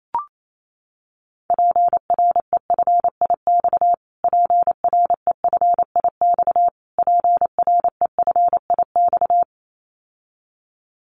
Morse code, heard as a keyed tone, about 28 wpm, 700 Hz. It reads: PREFIX PREFIX PREFIX
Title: Callsign Prefixes for Top 25 DXCC - Repeated 3x - 28wpm